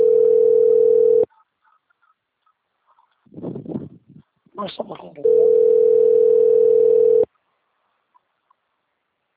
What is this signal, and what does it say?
Ringback tone (North America), 440 + 480 Hz, with speech.
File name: out- - - - - .23761